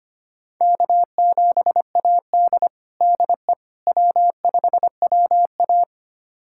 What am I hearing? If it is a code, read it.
Morse code, heard as a keyed tone, about 25 wpm, 700 Hz. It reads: K7AD DE W5WA
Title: USA Callsign DE Callsign - 4 Characters - 25wpm